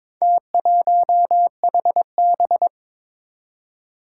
Morse code, heard as a keyed tone, about 22 wpm, 700 Hz. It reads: T1HB
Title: Four Challenging Letter-Number Combinations 22wpm